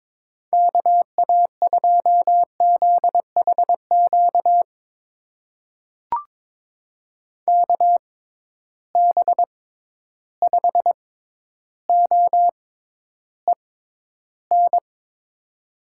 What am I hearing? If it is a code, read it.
Morse code, heard as a keyed tone, about 22 wpm, spaced as for 11 wpm, 700 Hz. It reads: KA2ZHQ KB5OEN
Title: USA Call Signs - 6 Characters - 6x Character Spacing - 22wpm